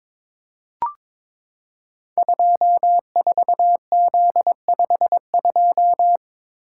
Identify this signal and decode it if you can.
Morse code, heard as a keyed tone, about 22 wpm, 700 Hz. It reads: 24Z52
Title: Code Groups - 5 Letter:Numbers - 22wpm